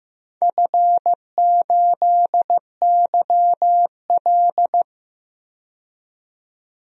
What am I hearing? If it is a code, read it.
Morse code, heard as a keyed tone, about 15 wpm, 700 Hz. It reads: F8YL
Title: DXCC-100 Call Signs - 4 Characters - 15wpm